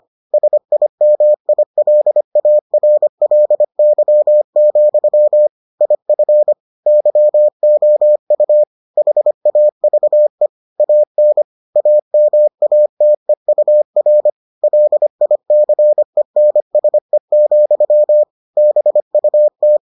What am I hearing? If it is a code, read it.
Morse code, heard as a keyed tone, about 25 wpm, 600 Hz. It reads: SIMILARLY, IF YOU HAVE AN AMATEUR LICENSE, BUT